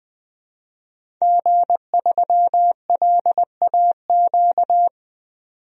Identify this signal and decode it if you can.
Morse code, heard as a keyed tone, about 20 wpm, 700 Hz. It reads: G3LAQ
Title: DXCC-100 Call Signs - 5 Characters - 20wpm